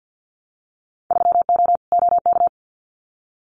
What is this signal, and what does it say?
Morse code, heard as a keyed tone, about 48 wpm, 700 Hz. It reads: 3O OK
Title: Winter Field Day Exchanges 50wpm